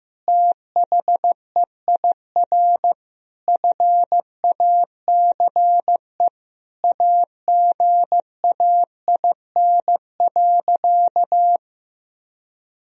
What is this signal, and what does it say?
Morse code, heard as a keyed tone, about 15 wpm, 700 Hz. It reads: THEIR FACE AGAIN.
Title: Sentences from Top 1000 Words - Review Part 1 - 15wpm